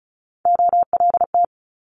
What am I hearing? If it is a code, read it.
Morse code, heard as a keyed tone, about 36 wpm, 700 Hz. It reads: OLT
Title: Multi-letter Phonemes Quick Fire 35wpm